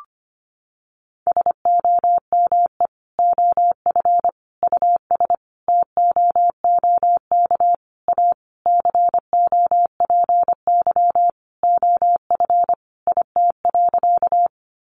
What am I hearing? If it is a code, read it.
Morse code, heard as a keyed tone, about 25 wpm, 700 Hz. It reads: SOME OF US TOOK A COPY OF IT.
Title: Sentences from Top 1400 Words 25wpm